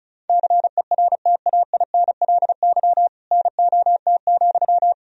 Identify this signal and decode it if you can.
Morse code, heard as a keyed tone, about 36 wpm, 700 Hz. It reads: CERTAINLY NOT,
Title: Sentences from Top 1000 Words - Review Part 1 - 35wpm